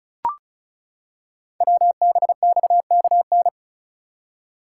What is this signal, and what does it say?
Morse code, heard as a keyed tone, about 36 wpm, 700 Hz. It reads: WBXKN
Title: Code Groups - 5 Letters - 35wpm